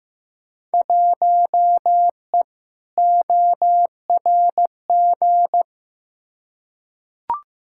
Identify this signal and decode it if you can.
Morse code, heard as a keyed tone, about 15 wpm, 700 Hz. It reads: 1E ORG
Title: Field Day Exchanges - 15wpm